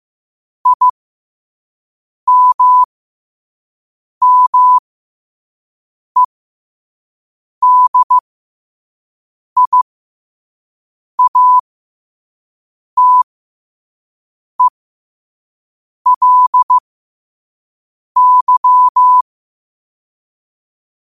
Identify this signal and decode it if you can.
Morse code, heard as a keyed tone, about 15 wpm, spaced as for 5 wpm, 1000 Hz. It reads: IMMEDIATELY
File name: qotdfiles / 5wpm